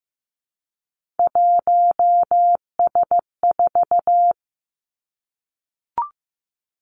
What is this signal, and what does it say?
Morse code, heard as a keyed tone, about 15 wpm, 700 Hz. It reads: 1S4
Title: Three Challenging Letter-Number Combinations 15wpm